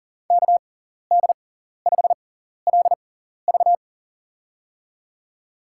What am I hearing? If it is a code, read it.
Morse code, heard as a keyed tone, about 40 wpm, spaced as for 14 wpm, 700 Hz. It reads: KD5LV
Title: USA Call Signs - 5 Characters - 6x Character Spacing - 40wpm